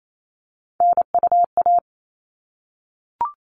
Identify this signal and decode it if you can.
Morse code, heard as a keyed tone, about 28 wpm, 700 Hz. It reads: NUA
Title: All Cut Numbers - 3-Digit 28wpm